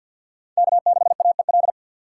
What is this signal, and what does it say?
Morse code, heard as a keyed tone, about 46 wpm, 700 Hz. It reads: K6AEL
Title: USA Call Signs - 5 Characters - 3x Character Spacing - 50wpm